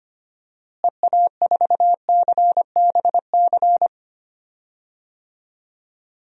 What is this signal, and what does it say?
Morse code, heard as a keyed tone, about 25 wpm, 700 Hz. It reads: EA4CBC
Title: DXCC-100 Call Signs - 6 Characters - Repeated 2x - 25wpm